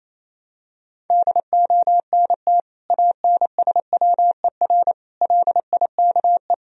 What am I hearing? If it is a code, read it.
Morse code, heard as a keyed tone, about 27 wpm, 700 Hz. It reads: DONT ANSWER LIKE
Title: Sentences from Top 400 Words and Abbreviations 28wpm